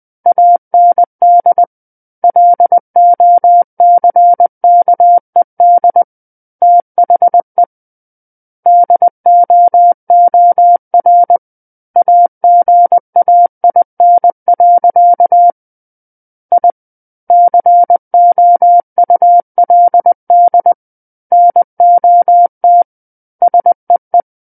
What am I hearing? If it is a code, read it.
Morse code, heard as a keyed tone, about 20 wpm, 700 Hz. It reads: AND LOCKED THE DOOR AGAIN. I COULD NOT SEE
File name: 20 WPM